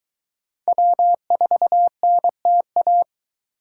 Morse code, heard as a keyed tone, about 22 wpm, 700 Hz. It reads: W4NTA